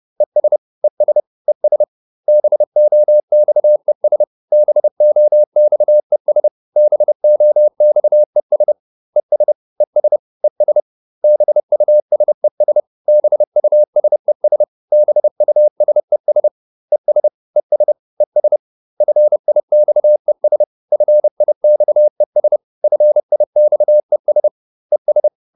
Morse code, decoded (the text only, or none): ES ES ES BOXES BOXES BOXES ES ES ES BUSES BUSES BUSES ES ES ES FIXES FIXES FIXES ES